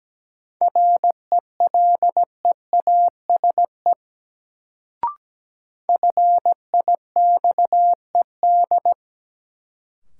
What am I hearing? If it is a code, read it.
Morse code, heard as a keyed tone, about 17 wpm, 700 Hz. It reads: RELEASE FIXED
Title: Sentences from Top 1100 Words - New Word Drill Alt - 17wpm